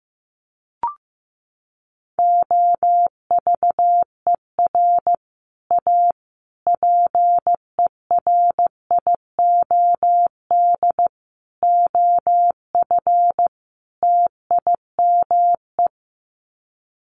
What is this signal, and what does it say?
Morse code, heard as a keyed tone, about 15 wpm, 700 Hz. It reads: OVER A PERIOD OF TIME
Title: Sets of 5 Words Encore2 15wpm